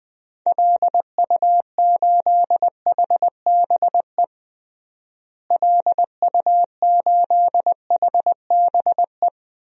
Morse code, heard as a keyed tone, about 20 wpm, 700 Hz. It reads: LU8HBE LU8HBE
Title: DXCC-100 Call Signs - 6 Characters - Repeated 3x - 20wpm